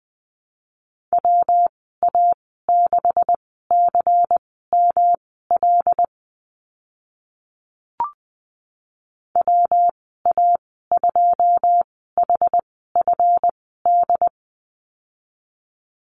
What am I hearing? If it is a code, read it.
Morse code, heard as a keyed tone, about 20 wpm, spaced as for 14 wpm, 700 Hz. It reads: WA6CML WA2HFD
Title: 193 Lesson 31g - Letter v - Call Signs 20wpm